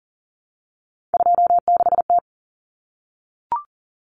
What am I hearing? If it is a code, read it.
Morse code, heard as a keyed tone, about 40 wpm, 700 Hz. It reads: 26T
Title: Common Cut Numbers - 3-Digit 40wpm